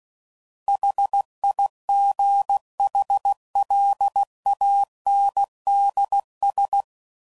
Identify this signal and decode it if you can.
Morse code, heard as a keyed tone, about 16 wpm, 800 Hz. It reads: HIGHLANDS
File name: CW Code Groups @15WPM